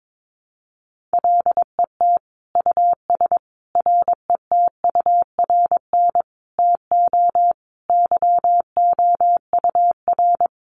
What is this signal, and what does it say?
Morse code, heard as a keyed tone, about 22 wpm, 700 Hz. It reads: LET US RETURN TO YOUR
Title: Sentences from Top 800 Words 22wpm